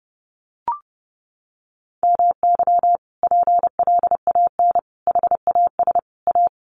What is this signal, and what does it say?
Morse code, heard as a keyed tone, about 30 wpm, 700 Hz. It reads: MY PLAN HAS A
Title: Sentences from Top 1300 Words 30wpm